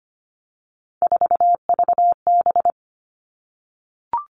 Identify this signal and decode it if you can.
Morse code, heard as a keyed tone, about 25 wpm, 700 Hz. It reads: 4VB
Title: Three Challenging Letter-Number Combinations 25wpm